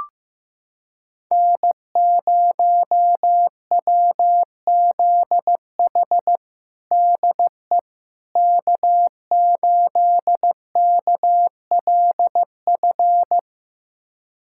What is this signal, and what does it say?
Morse code, heard as a keyed tone, about 16 wpm, 700 Hz. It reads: N0WZH DE K8KLF
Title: USA Callsign DE Callsign - 5 Characters - 15wpm